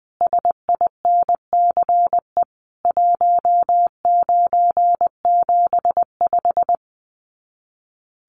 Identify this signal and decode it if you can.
Morse code, heard as a keyed tone, about 20 wpm, 700 Hz. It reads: SINCE 1975
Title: Common QSO Phrases 20wpm